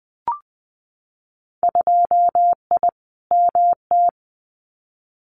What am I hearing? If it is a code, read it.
Morse code, heard as a keyed tone, about 20 wpm, 700 Hz. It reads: 2I MT
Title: Winter Field Day Exchanges Encore 20wpm